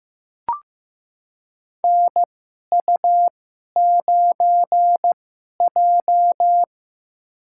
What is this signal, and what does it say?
Morse code, heard as a keyed tone, about 15 wpm, spaced as for 11 wpm, 700 Hz. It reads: NU9J
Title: USA Call Signs - 4 Characters - 2x Character Spacing - 15wpm